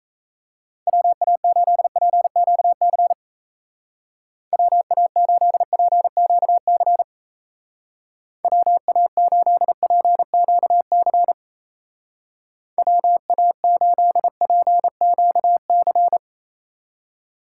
Morse code, heard as a keyed tone, about 34 wpm, 700 Hz. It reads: WA8PQC WA8PQC WA8PQC WA8PQC
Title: Speed Racing - USA 2x3 Call Signs 28wpm